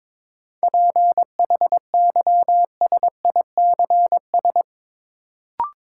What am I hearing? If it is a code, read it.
Morse code, heard as a keyed tone, about 22 wpm, 700 Hz. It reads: PHYSICS
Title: Sentences from Top 1500 Words - New Word Drill Alt - 22wpm